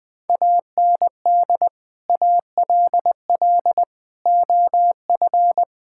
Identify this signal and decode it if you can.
Morse code, heard as a keyed tone, about 20 wpm, 700 Hz. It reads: AND ALL OF
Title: Build A Phrase - 5-Word Phrases - 20wpm